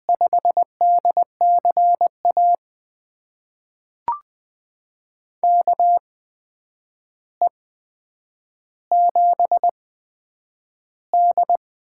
Morse code, heard as a keyed tone, about 20 wpm, spaced as for 9 wpm, 700 Hz. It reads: 5DCA KE7D
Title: USA Call Signs - 6 Characters - 8x Character Spacing - 20wpm